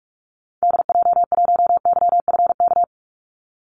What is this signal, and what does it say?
Morse code, heard as a keyed tone, about 44 wpm, 700 Hz. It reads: DJ1YFK
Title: Call Signs Repeated 2x - 45wpm